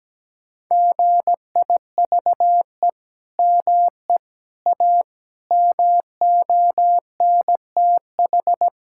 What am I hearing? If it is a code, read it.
Morse code, heard as a keyed tone, about 17 wpm, 700 Hz. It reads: GIVE ME A MONTH
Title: Sentences from Top 1000 Words - Review Part 5 - 17wpm